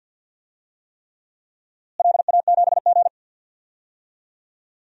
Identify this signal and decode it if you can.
Morse code, heard as a keyed tone, about 52 wpm, 700 Hz. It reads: LA7G
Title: DXCC-100 Call Signs - 4 Characters - Repeated 3x - 50wpm